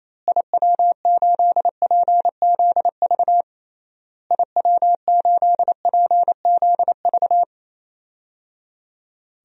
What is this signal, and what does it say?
Morse code, heard as a keyed tone, about 28 wpm, 700 Hz. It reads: IW8PZV IW8PZV
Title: DXCC-100 Call Signs - 6 Characters - Repeated 3x - 28wpm